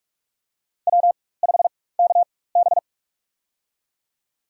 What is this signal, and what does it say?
Morse code, heard as a keyed tone, about 44 wpm, spaced as for 21 wpm, 700 Hz. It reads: W5KB
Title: USA Call Signs - 4 Characters - 4x Character Spacing - 45wpm